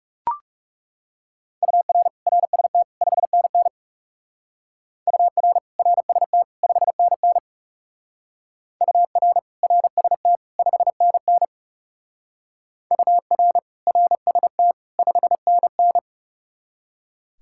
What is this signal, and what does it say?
Morse code, heard as a keyed tone, about 38 wpm, 700 Hz. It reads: UR RST 5NN UR RST 5NN UR RST 5NN UR RST 5NN